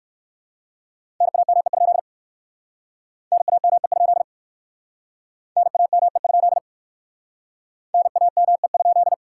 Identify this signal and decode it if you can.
Morse code, heard as a keyed tone, about 60 wpm, 700 Hz. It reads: NAME? NAME? NAME? NAME?